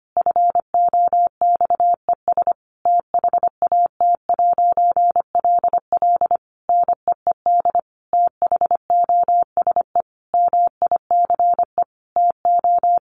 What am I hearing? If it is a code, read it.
Morse code, heard as a keyed tone, about 25 wpm, 700 Hz. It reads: FOXES THAT'LL NEED THOSE MICE TO
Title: A Sound Of Thunder 25wpm